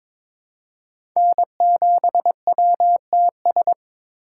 Morse code, heard as a keyed tone, about 22 wpm, 700 Hz. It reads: N7WTS
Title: USA Call Signs - 5 Characters - 3x Character Spacing - 22wpm